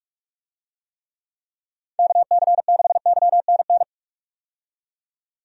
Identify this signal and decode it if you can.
Morse code, heard as a keyed tone, about 46 wpm, 700 Hz. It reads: KC6YNN
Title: USA Call Signs - 6 Characters - 45wpm